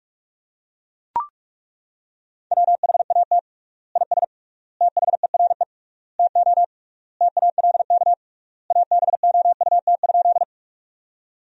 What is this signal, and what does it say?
Morse code, heard as a keyed tone, about 41 wpm, 700 Hz. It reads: WHAT IS THERE TO TALK ABOUT?
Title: 3x Word-Spacing - Sentences from Top 400 Words - 45wpm